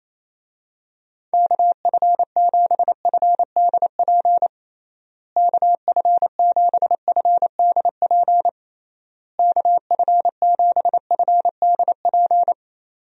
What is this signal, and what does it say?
Morse code, heard as a keyed tone, about 28 wpm, 700 Hz. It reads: KF7FDP KF7FDP KF7FDP
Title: USA Call Signs - 6 Characters - Repeated 3x - 28wpm